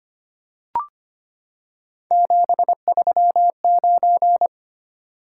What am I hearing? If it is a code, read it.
Morse code, heard as a keyed tone, about 25 wpm, 700 Hz. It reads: 739
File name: Three Letter-Number Combinations - 25wpm